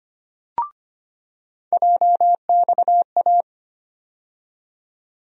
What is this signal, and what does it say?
Morse code, heard as a keyed tone, about 25 wpm, 700 Hz. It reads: JXA
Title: USA Call Sign Postfixes - 3 Letters 25wpm